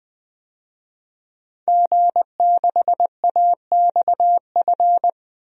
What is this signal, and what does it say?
Morse code, heard as a keyed tone, about 20 wpm, 700 Hz. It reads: G6AXF
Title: DXCC-100 Call Signs - 5 Characters - 20wpm